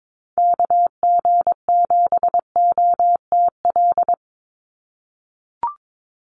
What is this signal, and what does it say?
Morse code, heard as a keyed tone, about 22 wpm, 700 Hz. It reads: KG7OTL
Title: USA Call Signs 22wpm